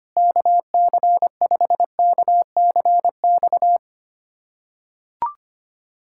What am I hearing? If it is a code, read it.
Morse code, heard as a keyed tone, about 25 wpm, 700 Hz. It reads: KC5KCX
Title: USA Callsign DE Callsign - 6 Characters - 25wpm